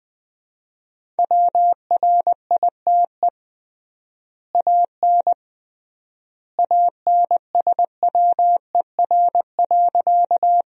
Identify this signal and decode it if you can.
Morse code, heard as a keyed tone, about 20 wpm, 700 Hz. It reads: WRITE AN ANSWER.
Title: 3x Word-Spacing - Sentences from Top 300 Words - 20wpm